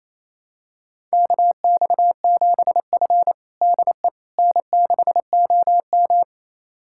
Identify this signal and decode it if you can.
Morse code, heard as a keyed tone, about 27 wpm, 700 Hz. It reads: KX7F DE N6OM
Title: USA Callsign DE Callsign - 4 Characters - 28wpm